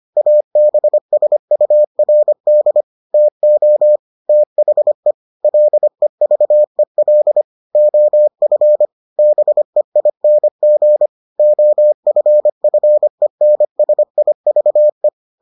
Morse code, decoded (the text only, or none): ABSURD TO THE LEVEL OF BEING OFFENSIVE